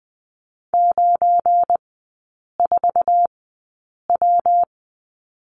Morse code, heard as a keyed tone, about 20 wpm, spaced as for 8 wpm, 700 Hz. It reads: 94W
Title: 172 Lesson 28e - Number 9 - Three Characters 20wpm